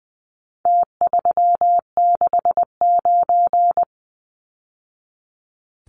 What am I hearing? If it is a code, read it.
Morse code, heard as a keyed tone, about 20 wpm, 700 Hz. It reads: T369
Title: Numbers - Contest Style - 4-Digit 20wpm